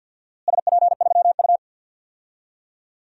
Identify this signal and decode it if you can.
Morse code, heard as a keyed tone, about 47 wpm, 700 Hz. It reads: SP3U
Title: DXCC-100 Call Signs - 4 Characters - Repeated 2x - 50wpm